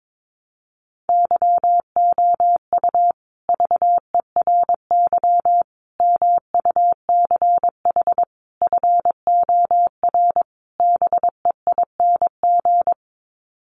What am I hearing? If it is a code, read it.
Morse code, heard as a keyed tone, about 22 wpm, 700 Hz. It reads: YOU VERY MUCH FOR BEING
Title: Sets of 5 Words Encore 22wpm